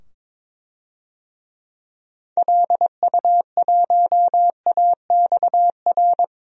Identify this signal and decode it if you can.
Morse code, heard as a keyed tone, about 22 wpm, 700 Hz. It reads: LU1AXR
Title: DXCC-100 Call Signs - 6 Characters - 22wpm